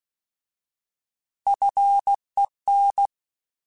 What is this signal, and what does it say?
Morse code, heard as a keyed tone, about 16 wpm, 800 Hz. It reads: FEN
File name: Alphabetical / Alphabetical mp3s / Fs @15